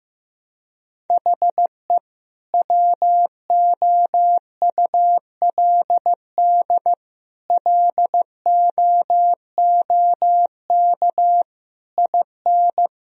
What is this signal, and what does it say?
Morse code, heard as a keyed tone, about 15 wpm, 700 Hz. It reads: HE WOULD LOOK IN
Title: Sentences from Top 1000 Words - Review Part 3 - 15wpm